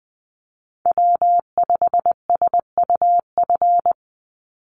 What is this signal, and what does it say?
Morse code, heard as a keyed tone, about 20 wpm, 700 Hz. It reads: W5SUF
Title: USA Call Signs 20wpm